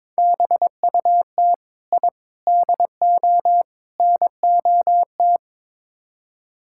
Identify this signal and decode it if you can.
Morse code, heard as a keyed tone, about 22 wpm, 700 Hz. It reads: BUT I DO NOT
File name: practice / 22wpm / Build A Phrase - 5-Word Phrases - 22wpm